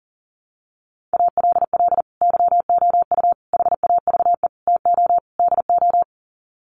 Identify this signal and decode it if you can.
Morse code, heard as a keyed tone, about 38 wpm, 700 Hz. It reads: ALL YOU HAVE TO DO
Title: Sets of 5 Words 40wpm